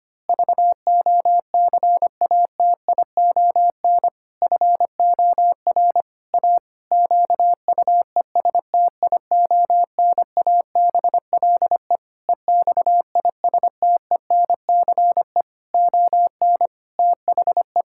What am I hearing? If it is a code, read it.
Morse code, heard as a keyed tone, about 25 wpm, 700 Hz. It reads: VOCATION FOR A QUESTIONABLE EXISTENCE ON THE
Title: Pangram Challenge v2 - 25wpm